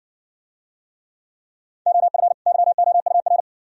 Morse code, heard as a keyed tone, about 54 wpm, 690 Hz. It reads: OF YOUR